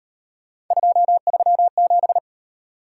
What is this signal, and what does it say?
Morse code, heard as a keyed tone, about 34 wpm, 700 Hz. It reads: 237